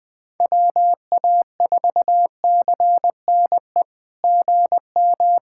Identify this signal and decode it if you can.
Morse code, heard as a keyed tone, about 20 wpm, 700 Hz. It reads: WA4CNE GM
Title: SOTA QSO Practice 20wpm